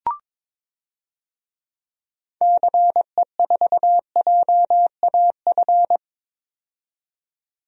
Morse code, heard as a keyed tone, about 22 wpm, 700 Hz. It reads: CE4JAF
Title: DXCC-100 Call Signs - 6 Characters - 22wpm